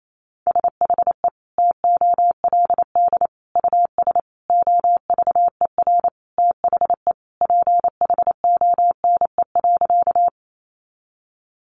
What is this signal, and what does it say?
Morse code, heard as a keyed tone, about 28 wpm, 700 Hz. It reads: SHE TOLD US OVER THE PHONE.
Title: Sentences from Top 1100 Words 28wpm